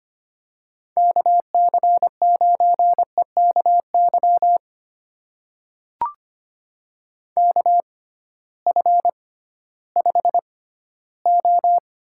Morse code, heard as a keyed tone, about 25 wpm, spaced as for 14 wpm, 700 Hz. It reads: KC9EKY KF5O
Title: USA Call Signs - 6 Characters - 6x Character Spacing - 25wpm